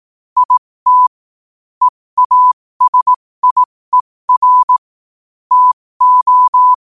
Morse code, heard as a keyed tone, about 18 wpm, spaced as for 15 wpm, 1000 Hz. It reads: IT EASIER TO
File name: qotdfiles / 15wpm